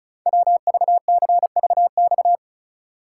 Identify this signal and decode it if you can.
Morse code, heard as a keyed tone, about 36 wpm, 700 Hz. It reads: WVCVX
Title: Code Groups - 5 Letter:Numbers - 35wpm